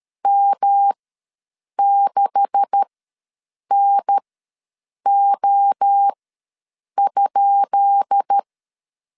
Morse code, heard as a keyed tone, about 13 wpm, spaced as for 7 wpm, 800 Hz. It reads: M6NO?
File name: LEZIONI / 10 SIGNS / 10D COMA ws 12wpm 3618